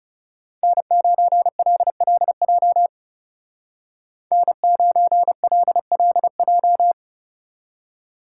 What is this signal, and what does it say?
Morse code, heard as a keyed tone, about 39 wpm, 700 Hz. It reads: N9LLJ N9LLJ